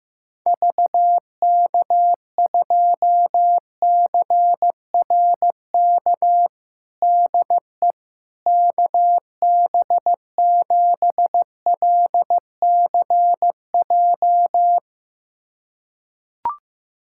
Morse code, heard as a keyed tone, about 15 wpm, 700 Hz. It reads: VK2CRK DE KB7LCJ
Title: USA Callsign DE Callsign 15wpm